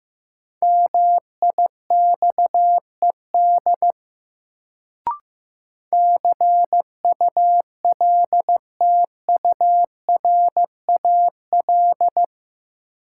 Morse code, heard as a keyed tone, about 15 wpm, 700 Hz. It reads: MIXED CULTURAL
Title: Sentences from Top 1500 Words - New Word Drill Alt - 15wpm